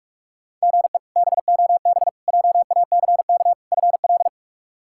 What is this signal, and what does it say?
Morse code, heard as a keyed tone, about 46 wpm, 700 Hz. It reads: GE BOB JACK FL
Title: K1USN SST QSO Practice - 45wpm